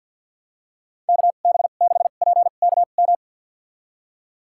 Morse code, heard as a keyed tone, about 52 wpm, spaced as for 36 wpm, 700 Hz. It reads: KB6PKM
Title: USA Call Signs - 2x Character Spacing - 50wpm